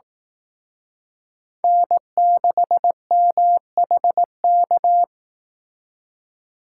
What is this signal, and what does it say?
Morse code, heard as a keyed tone, about 18 wpm, 700 Hz. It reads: N6MHK
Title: Speed Racing - USA 1x3 Call Signs 15wpm